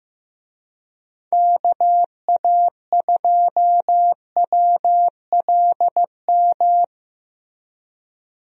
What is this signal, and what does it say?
Morse code, heard as a keyed tone, about 15 wpm, 700 Hz. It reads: KA2WLM
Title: USA Call Signs - 6 Characters - Repeated 3x - 15wpm